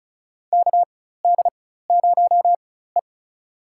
Morse code, heard as a keyed tone, about 34 wpm, spaced as for 16 wpm, 700 Hz. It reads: KD0E
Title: USA Call Signs - 4 Characters - 4x Character Spacing - 35wpm